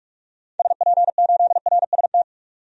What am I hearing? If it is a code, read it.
Morse code, heard as a keyed tone, about 46 wpm, 700 Hz. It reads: SP8RST
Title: DXCC-100 Call Signs - 6 Characters - 45wpm